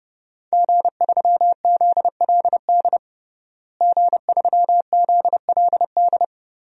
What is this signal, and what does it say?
Morse code, heard as a keyed tone, about 32 wpm, 700 Hz. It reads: G3ZLD G3ZLD
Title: DXCC-100 Call Signs - 5 Characters - Repeated 3x - 30wpm